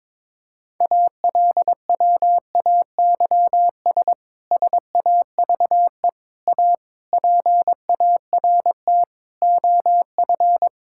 Morse code, heard as a keyed tone, about 22 wpm, 700 Hz. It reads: ALWAYS SAVE A PART OF
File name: Difficult Sentences with Hints - 22wpm